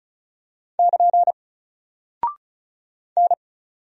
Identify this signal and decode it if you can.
Morse code, heard as a keyed tone, about 36 wpm, 700 Hz. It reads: <KN> N
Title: Single Character (Letter:Number:Punctuation:Prosign) 35wpm